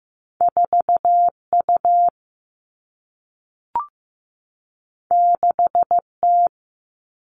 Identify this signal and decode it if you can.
Morse code, heard as a keyed tone, about 15 wpm, 700 Hz. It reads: 4U 6T
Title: All Cut Numbers - 2-Digit 15wpm